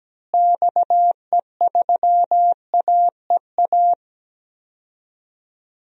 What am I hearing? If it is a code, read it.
Morse code, heard as a keyed tone, about 17 wpm, 700 Hz. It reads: XE3AEA
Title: DXCC-100 Call Signs - 6 Characters - Repeated 3x - 17wpm